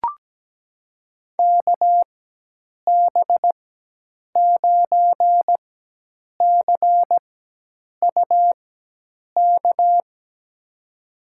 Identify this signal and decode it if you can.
Morse code, heard as a keyed tone, about 17 wpm, spaced as for 8 wpm, 700 Hz. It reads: KB9CUK
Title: USA Call Signs - 6 Characters - 4x Character Spacing - 17wpm